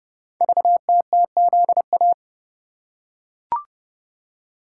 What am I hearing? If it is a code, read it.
Morse code, heard as a keyed tone, about 30 wpm, 700 Hz. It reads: VTTZA
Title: Code Groups - 5 Letters - 30wpm